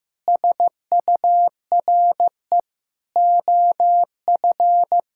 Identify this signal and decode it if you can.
Morse code, heard as a keyed tone, about 16 wpm, 700 Hz. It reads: SURE OF
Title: Sentences from Top 1600 Words 15wpm